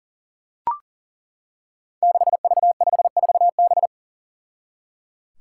Morse code, heard as a keyed tone, about 39 wpm, 700 Hz. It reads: 6V54B
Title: Five Challenging Letter-Number Combinations 40wpm